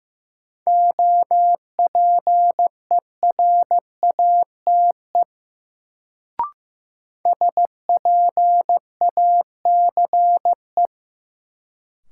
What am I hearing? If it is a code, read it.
Morse code, heard as a keyed tone, about 15 wpm, 700 Hz. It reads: OPERATE SPACE
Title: Sentences from Top 1500 Words - New Word Drill Alt - 15wpm